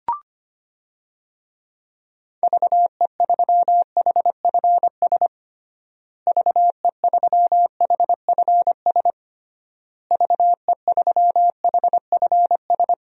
Morse code, heard as a keyed tone, about 25 wpm, 700 Hz. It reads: VE3HFS VE3HFS VE3HFS
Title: DXCC-100 Call Signs - 6 Characters - Repeated 3x - 25wpm